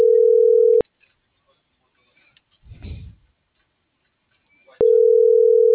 Ringback tone (North America), 440 + 480 Hz.